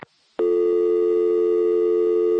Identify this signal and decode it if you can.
Dial tone (North America), 350 + 440 Hz.
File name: phone numbers / calls / building superintendent var1